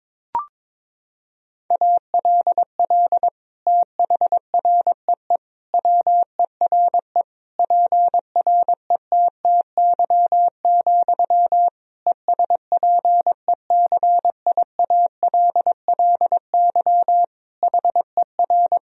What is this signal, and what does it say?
Morse code, heard as a keyed tone, about 21 wpm, 700 Hz. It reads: ALL THREE WERE PRETTY, ESPECIALLY HER
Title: Sentences from Top 1000 Words - Review Part 4 - 22wpm